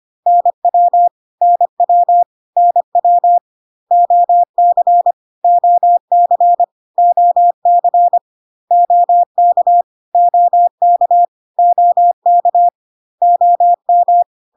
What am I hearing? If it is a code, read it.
Morse code, heard as a keyed tone, about 25 wpm, 700 Hz. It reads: NW NW NW OC OC OC OK OK OK OM
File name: cw / CW common abbreviations alphabetical repeated 3 times 25 25wpm ik1hge